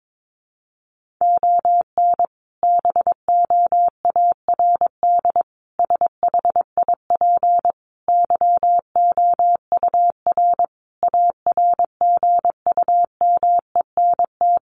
Morse code, heard as a keyed tone, about 22 wpm, 700 Hz. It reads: ON BOARD SHIP YOUR ARGUMENT
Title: Sentences from Top 1400 Words 22wpm